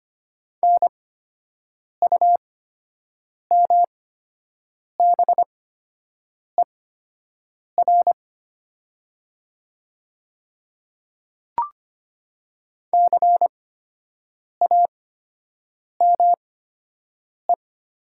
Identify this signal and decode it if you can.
Morse code, heard as a keyed tone, about 25 wpm, spaced as for 7 wpm, 700 Hz. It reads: NUMBER CAME
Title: 8x Character Spacing - Top 200 Words - 25wpm